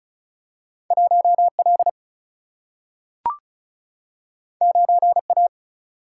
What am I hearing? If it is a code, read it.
Morse code, heard as a keyed tone, about 36 wpm, 700 Hz. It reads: 1 L 9 A
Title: Single-Letter Single-Number 35wpm